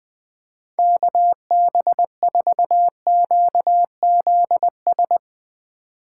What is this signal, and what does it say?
Morse code, heard as a keyed tone, about 20 wpm, 700 Hz. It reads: KB4QZS
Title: Speed Racing - USA 2x3 Call Signs 17wpm